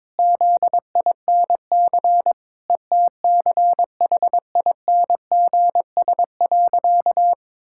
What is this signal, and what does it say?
Morse code, heard as a keyed tone, about 22 wpm, 700 Hz. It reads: ZINC ETCHINGS.
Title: Pangram Challenge v2 - 22wpm